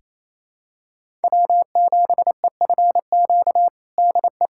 Morse code, heard as a keyed tone, about 28 wpm, 700 Hz. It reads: W7EFQ DE
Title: USA Callsign DE Callsign - 5 Characters - 28wpm